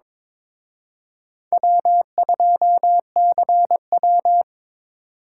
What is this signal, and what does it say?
Morse code, heard as a keyed tone, about 26 wpm, 700 Hz. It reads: W2CW
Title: Speed Racing - USA 1x2 Call Signs 22wpm